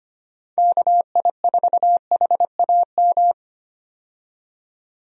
Morse code, heard as a keyed tone, about 25 wpm, 700 Hz. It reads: KI4HAM
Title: USA Call Signs - 6 Characters - 25wpm